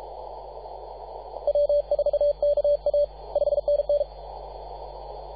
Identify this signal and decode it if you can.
Morse code, heard as a keyed tone, about 33 wpm, 580 Hz, with radio noise. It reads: W4KA 5NN
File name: posts / 2015-sept-24-2118z-t2gc-18-071-cw